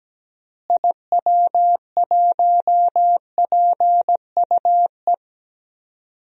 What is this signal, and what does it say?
Morse code, heard as a keyed tone, about 17 wpm, 700 Hz. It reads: IW1PUE
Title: DXCC-100 Call Signs - 6 Characters - Repeated 3x - 17wpm